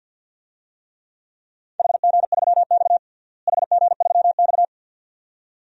Morse code, heard as a keyed tone, about 51 wpm, 700 Hz. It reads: HG3X HG3X
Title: DXCC-100 Call Signs - 4 Characters - Repeated 2x - 50wpm